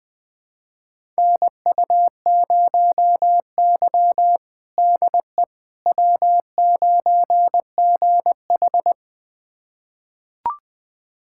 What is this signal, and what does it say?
Morse code, heard as a keyed tone, about 20 wpm, 700 Hz. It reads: NU0Y DE W9GH
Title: USA Callsign DE Callsign - 4 Characters - 20wpm